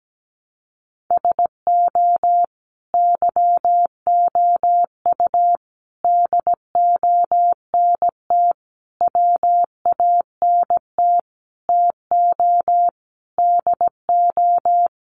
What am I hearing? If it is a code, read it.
Morse code, heard as a keyed tone, about 17 wpm, 700 Hz. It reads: SO YOU DONT WANT TO DO
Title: Sentences from Top 1000 Words 17wpm